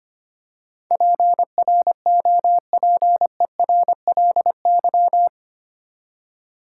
Morse code, heard as a keyed tone, about 25 wpm, 700 Hz. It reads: PROPERLY
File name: Eight Letter Words - 25wpm